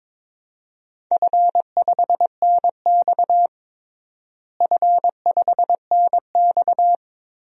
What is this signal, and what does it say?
Morse code, heard as a keyed tone, about 22 wpm, 700 Hz. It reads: F5NX F5NX
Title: DXCC-100 Call Signs - 4 Characters - Repeated 3x - 22wpm